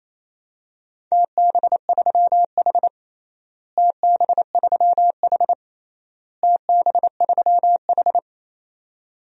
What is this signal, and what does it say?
Morse code, heard as a keyed tone, about 28 wpm, 700 Hz. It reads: TB3H TB3H TB3H
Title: DXCC-100 Call Signs - 4 Characters - Repeated 3x - 28wpm